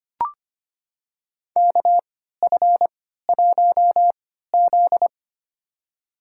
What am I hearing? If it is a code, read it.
Morse code, heard as a keyed tone, about 25 wpm, spaced as for 14 wpm, 700 Hz. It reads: KF1Z
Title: USA Call Signs - 4 Characters - 3x Character Spacing - 25wpm